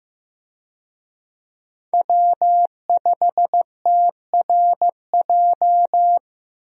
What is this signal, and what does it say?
Morse code, heard as a keyed tone, about 15 wpm, 700 Hz. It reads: W5TRJ
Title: USA Call Signs - 5 Characters - Repeated 3x - 15wpm